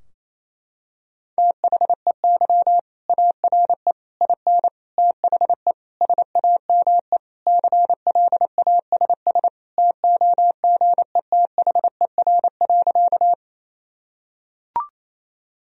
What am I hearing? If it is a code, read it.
Morse code, heard as a keyed tone, about 28 wpm, 700 Hz. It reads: THEY ARE IN THE SAME CLASS TOGETHER.
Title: Sentences from Top 1000 Words - Review Part 2 - 28wpm